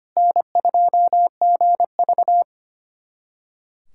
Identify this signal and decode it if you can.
Morse code, heard as a keyed tone, about 25 wpm, 700 Hz. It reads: N2GV
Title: USA Callsign DE Callsign - 4 Characters - 25wpm